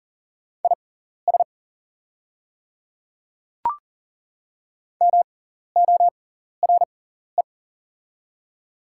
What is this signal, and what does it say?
Morse code, heard as a keyed tone, about 40 wpm, spaced as for 14 wpm, 690 Hz. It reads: IS MORE
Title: 6x Character Spacing - Top 100 Words - 40wpm